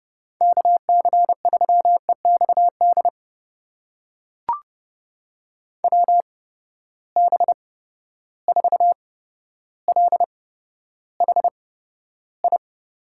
Morse code, heard as a keyed tone, about 30 wpm, spaced as for 13 wpm, 700 Hz. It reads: KC3EXD WB4LHI
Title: USA Call Signs - 6 Characters - 8x Character Spacing - 30wpm